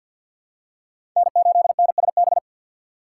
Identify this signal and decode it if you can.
Morse code, heard as a keyed tone, about 52 wpm, 700 Hz. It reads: N8NSB
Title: USA Call Signs - 5 Characters - 8x Character Spacing - 50wpm